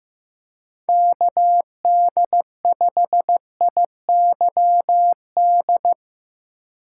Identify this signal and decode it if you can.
Morse code, heard as a keyed tone, about 15 wpm, 700 Hz. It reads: KD5IYD
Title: USA Call Signs - 6 Characters - 3x Character Spacing - 15wpm